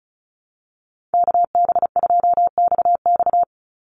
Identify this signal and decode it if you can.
Morse code, heard as a keyed tone, about 34 wpm, 700 Hz. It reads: KB2XX
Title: NA SOTA Chaser Callsigns 35wpm